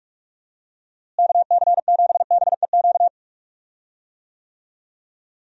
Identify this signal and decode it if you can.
Morse code, heard as a keyed tone, about 46 wpm, 700 Hz. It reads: KC7BEQ
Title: USA Call Signs - 6 Characters - Repeated 2x - 45wpm